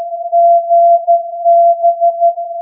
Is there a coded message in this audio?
Yes: Morse code from 0.3 to 2.3 s, heard as a keyed tone, about 13 wpm, 680 Hz. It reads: GB